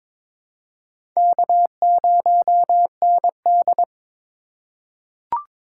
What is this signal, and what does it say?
Morse code, heard as a keyed tone, about 22 wpm, 700 Hz. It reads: K0ND
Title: USA Call Signs - 4x Character Spacing - 22wpm